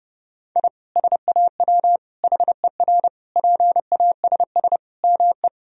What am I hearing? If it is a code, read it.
Morse code, heard as a keyed tone, about 32 wpm, 700 Hz. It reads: I SAW HER PASS ME
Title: Sentences from Top 1000 Words - Review Part 2 - 30wpm